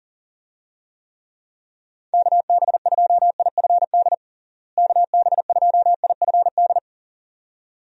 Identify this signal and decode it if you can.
Morse code, heard as a keyed tone, about 40 wpm, 700 Hz. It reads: KB2IFD KB2IFD
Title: USA Call Signs - 6 Characters - Repeated 2x - 40wpm